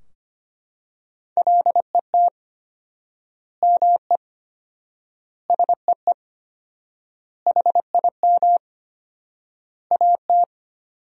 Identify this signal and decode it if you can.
Morse code, heard as a keyed tone, about 25 wpm, 700 Hz. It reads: LET ME SEE HIM AT
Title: 4x Word-Spacing - Sentences from Top 300 Words - 25wpm